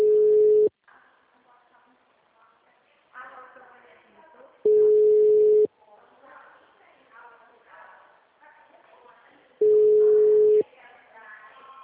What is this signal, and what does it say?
Ringback tone (Europe, 425 Hz), with speech.